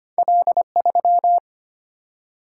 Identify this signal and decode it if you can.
Morse code, heard as a keyed tone, about 25 wpm, 700 Hz. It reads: L3
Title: Single-Letter Single-Number 25wpm